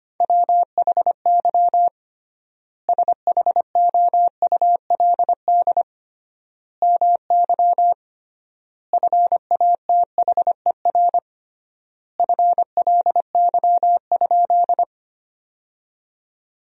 Morse code, heard as a keyed tone, about 25 wpm, 700 Hz. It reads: WHY SHOULD MY FATHER FLY?
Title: 3x Word-Spacing - Sentences from Top 500 Words - 25wpm